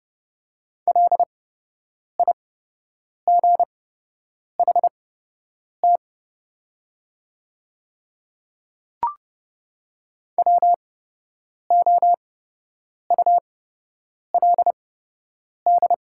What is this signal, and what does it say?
Morse code, heard as a keyed tone, about 29 wpm, spaced as for 8 wpm, 690 Hz. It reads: LIGHT WOULD
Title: 8x Character Spacing - Top 200 Words - 30wpm